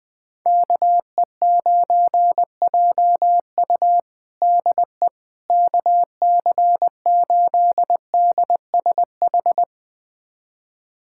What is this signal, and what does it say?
Morse code, heard as a keyed tone, about 20 wpm, 700 Hz. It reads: KE9JU DE KC8DSH
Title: USA Callsign DE Callsign 20wpm